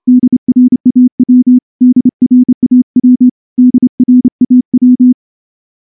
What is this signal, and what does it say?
Morse code, heard as a keyed tone, about 29 wpm, 250 Hz. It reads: DRAW DRAW DRAW
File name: morse-training-30WPM